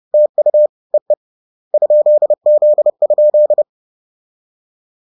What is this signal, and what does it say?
Morse code, heard as a keyed tone, about 30 wpm, 600 Hz. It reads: TU EE ?Z?
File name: 30wpm-7